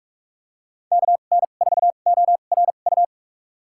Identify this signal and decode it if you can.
Morse code, heard as a keyed tone, about 46 wpm, spaced as for 33 wpm, 700 Hz. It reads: KN4ORU
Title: USA Call Signs - 6 Characters - 2x Character Spacing - 45wpm